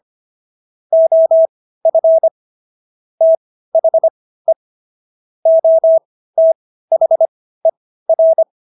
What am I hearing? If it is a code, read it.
Morse code, heard as a keyed tone, about 25 wpm, spaced as for 15 wpm, 650 Hz. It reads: OF THE OTHER